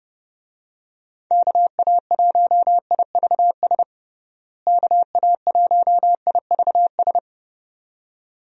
Morse code, heard as a keyed tone, about 30 wpm, 700 Hz. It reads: KA1IVS KA1IVS
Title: USA Call Signs - 6 Characters - Repeated 2x - 30wpm